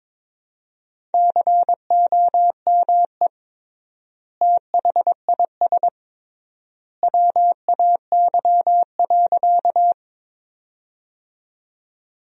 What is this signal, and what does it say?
Morse code, heard as a keyed tone, about 22 wpm, 700 Hz. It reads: COME THIS WAY.
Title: 3x Word-Spacing - Sentences from Top 100 Words - 22wpm